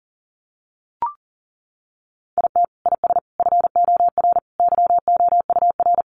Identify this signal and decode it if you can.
Morse code, heard as a keyed tone, about 37 wpm, 700 Hz. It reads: IT IS FOR YOUR
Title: Sentences from Top 700 Words 40wpm